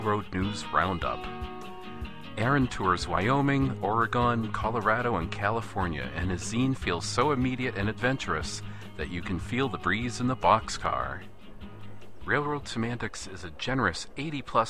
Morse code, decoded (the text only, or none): none